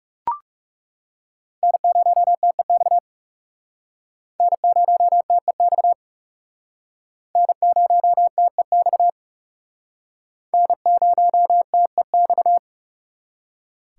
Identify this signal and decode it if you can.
Morse code, heard as a keyed tone, about 40 wpm, 700 Hz. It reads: N0TEX N0TEX N0TEX N0TEX